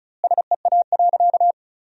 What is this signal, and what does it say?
Morse code, heard as a keyed tone, about 36 wpm, 700 Hz. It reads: SEA.
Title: 2x Word-Spacing - Sentences from Top 400 Words - 35wpm